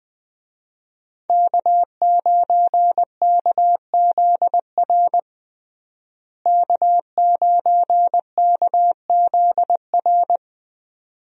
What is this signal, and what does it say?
Morse code, heard as a keyed tone, about 20 wpm, 700 Hz. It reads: K9KZR K9KZR
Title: USA Call Signs - 5 Characters - Repeated 2x - 20wpm